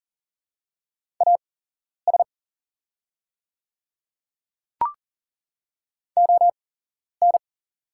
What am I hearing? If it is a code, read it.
Morse code, heard as a keyed tone, about 38 wpm, spaced as for 11 wpm, 690 Hz. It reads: AS ON
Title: 8x Character Spacing - Top 200 Words - 40wpm